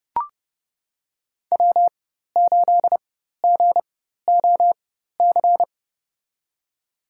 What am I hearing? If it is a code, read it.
Morse code, heard as a keyed tone, about 32 wpm, spaced as for 14 wpm, 700 Hz. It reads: W8GOC
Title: USA Call Signs - 5 Characters - 4x Character Spacing - 30wpm